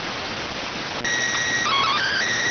SSTV (slow-scan television), PD 290.